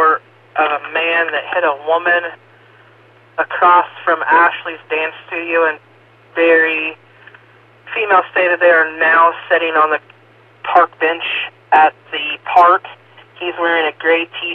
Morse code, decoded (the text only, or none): none